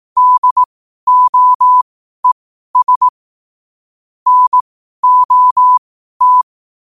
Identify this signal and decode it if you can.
Morse code, heard as a keyed tone, about 18 wpm, spaced as for 12 wpm, 1000 Hz. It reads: DOES NOT
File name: qotdfiles / 12wpm